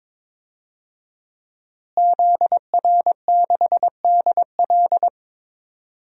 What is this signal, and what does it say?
Morse code, heard as a keyed tone, about 22 wpm, 700 Hz. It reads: ZR6DL
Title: DXCC-100 Call Signs - 5 Characters - Repeated 3x - 22wpm